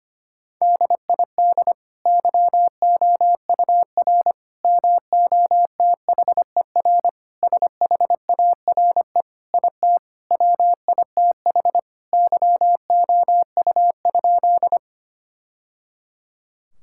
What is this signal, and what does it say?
Morse code, heard as a keyed tone, about 25 wpm, 700 Hz. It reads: DID YOUR MOTHER SHARE IT WITH YOU?
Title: Sentences from Top 1000 Words - Review Part 6 - 25wpm